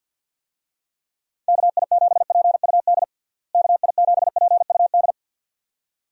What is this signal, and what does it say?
Morse code, heard as a keyed tone, about 51 wpm, 700 Hz. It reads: KI7PUD KI7PUD
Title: USA Call Signs - 6 Characters - Repeated 2x - 50wpm